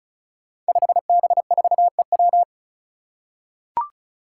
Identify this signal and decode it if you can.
Morse code, heard as a keyed tone, about 34 wpm, 700 Hz. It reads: 5B4EW
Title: Five Challenging Letter-Number Combinations 35wpm